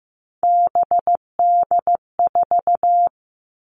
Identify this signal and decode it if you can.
Morse code, heard as a keyed tone, about 15 wpm, 700 Hz. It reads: BD4
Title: Callsign Prefixes for Top 100 DXCC - 15wpm